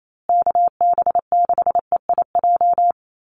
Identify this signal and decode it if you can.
Morse code, heard as a keyed tone, about 28 wpm, 700 Hz. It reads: KB6EIJ
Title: USA Call Signs Repeated 2x 28wpm